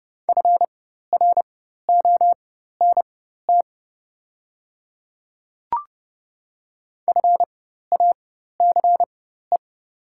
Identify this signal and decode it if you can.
Morse code, heard as a keyed tone, about 29 wpm, spaced as for 14 wpm, 690 Hz. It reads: FRONT FACE
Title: 4x Character Spacing - Top 500 Words - 30wpm